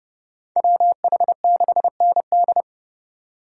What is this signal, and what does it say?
Morse code, heard as a keyed tone, about 37 wpm, 700 Hz. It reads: WH6ND
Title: Speed Racing - USA Call Signs 30wpm